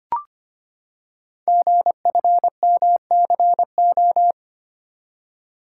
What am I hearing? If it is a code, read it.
Morse code, heard as a keyed tone, about 25 wpm, 700 Hz. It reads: GFMCO
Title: Code Groups - 5 Letters - 25wpm